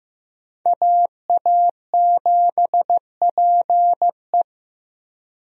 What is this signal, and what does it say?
Morse code, heard as a keyed tone, about 15 wpm, 700 Hz. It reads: AA7PE
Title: USA Call Signs - 5 Characters - Repeated 3x - 15wpm